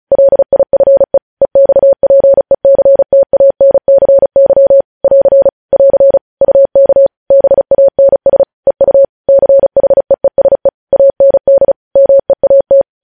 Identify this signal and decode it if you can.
Morse code, heard as a keyed tone, about 36 wpm, 550 Hz. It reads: LIFE EXPECTANCY <AR> <AR> UK BANS EU CHEESE AND MEAT
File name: podcast / News headlines-1x-35x35